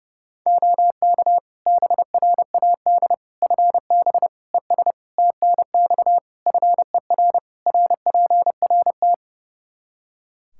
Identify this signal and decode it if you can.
Morse code, heard as a keyed tone, about 30 wpm, 700 Hz. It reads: OK BRAD FB ES TNX FER RPRT
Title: K1USN SST QSO Practice 30wpm